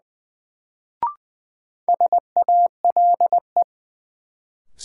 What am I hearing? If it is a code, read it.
Morse code, heard as a keyed tone, about 20 wpm, 700 Hz. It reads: SALE
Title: Sentences from Top 1400 Words - New Word Drill Alt - 20wpm